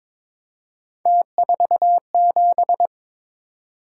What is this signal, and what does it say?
Morse code, heard as a keyed tone, about 22 wpm, 700 Hz. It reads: T47
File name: practice / 22wpm / Three Letter-Number Combinations - 22wpm